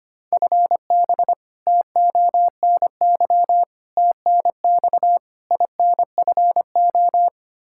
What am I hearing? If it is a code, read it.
Morse code, heard as a keyed tone, about 25 wpm, 700 Hz. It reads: FB TONY TNX INFO
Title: Practice QSO - Exchange 3 - More Info - 25wpm